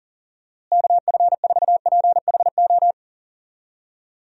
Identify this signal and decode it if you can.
Morse code, heard as a keyed tone, about 40 wpm, 700 Hz. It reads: KF4PHO
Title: USA Call Signs - 6 Characters - Repeated 3x - 40wpm